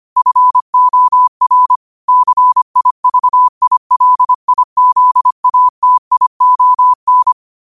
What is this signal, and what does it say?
Morse code, heard as a keyed tone, about 25 wpm, 1000 Hz. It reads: FOR CIVILIZATION
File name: qotdfiles / 25wpm